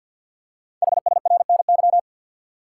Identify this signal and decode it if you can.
Morse code, heard as a keyed tone, about 48 wpm, 690 Hz. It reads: HSRNY
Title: Code Groups - 5 Letters - 50wpm